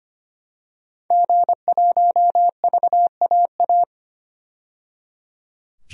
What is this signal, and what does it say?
Morse code, heard as a keyed tone, about 25 wpm, 700 Hz. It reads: G1VAA
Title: DXCC-100 Call Signs - 5 Characters - 25wpm